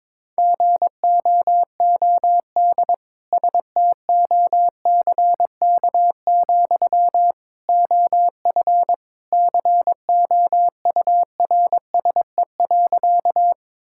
Morse code, heard as a keyed tone, about 22 wpm, 700 Hz. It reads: GOOD STOCK, OF COURSE.
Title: Sentences from Top 1000 Words - Review Part 5 - 22wpm